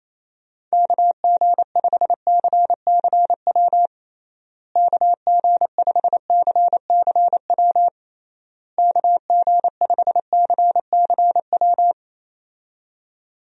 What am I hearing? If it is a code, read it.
Morse code, heard as a keyed tone, about 28 wpm, 700 Hz. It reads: KG5CCW KG5CCW KG5CCW
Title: USA Call Signs - 6 Characters - Repeated 3x - 28wpm